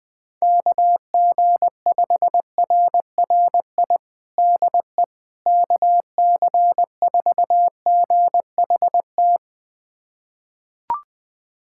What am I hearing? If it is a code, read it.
Morse code, heard as a keyed tone, about 21 wpm, 700 Hz. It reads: KG5RRI DE KC4GHT
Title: USA Callsign DE Callsign - 6 Characters - 20wpm